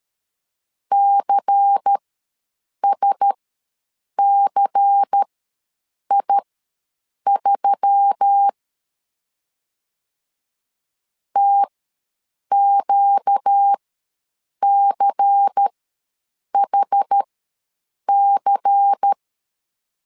Morse code, heard as a keyed tone, about 13 wpm, spaced as for 7 wpm, 800 Hz. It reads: CSCI3 TQCHC
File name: LEZIONI / 5 LBVC / 5 LBVC ws 12wpm 3159